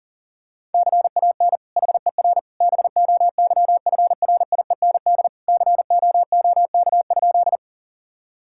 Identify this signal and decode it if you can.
Morse code, heard as a keyed tone, about 40 wpm, 700 Hz. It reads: CAN HER BOYFRIEND COOK?
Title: Sentences from Top 1000 Words - Review Part 5 - 40wpm